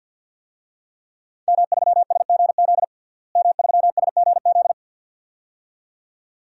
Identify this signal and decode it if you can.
Morse code, heard as a keyed tone, about 50 wpm, 700 Hz. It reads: M3SGZ M3SGZ
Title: DXCC-100 Call Signs - 5 Characters - Repeated 2x - 50wpm